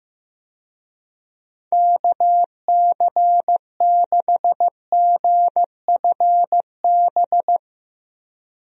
Morse code, heard as a keyed tone, about 15 wpm, 700 Hz. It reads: KC6GFB
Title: USA Call Signs - 6 Characters - 15wpm